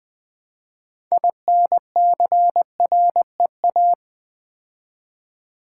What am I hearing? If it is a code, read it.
Morse code, heard as a keyed tone, about 20 wpm, 700 Hz. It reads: INCREA
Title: Build A Word - 2 Letters At A Time - 8-Letter Words - 20wpm